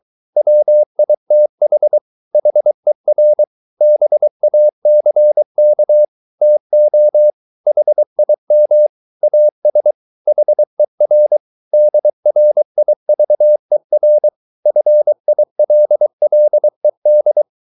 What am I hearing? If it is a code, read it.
Morse code, heard as a keyed tone, about 23 wpm, 600 Hz. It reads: WITH HER BACK TO HIM AS HER DRIVER FILLED